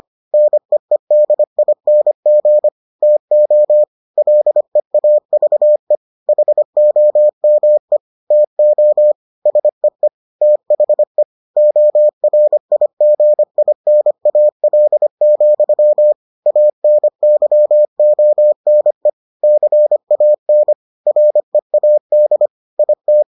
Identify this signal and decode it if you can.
Morse code, heard as a keyed tone, about 25 wpm, 600 Hz. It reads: NEEDING TO LEAVE HOME TO SEE THE ORIGINAL, ANYONE CAN READ IT